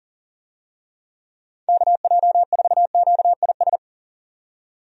Morse code, heard as a keyed tone, about 40 wpm, 700 Hz. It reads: KJ4QIS
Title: USA Call Signs - 6 Characters - 40wpm